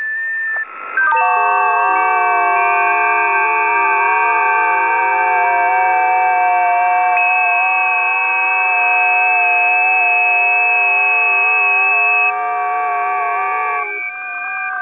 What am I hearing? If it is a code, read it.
FT8.
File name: 20m busy / test 13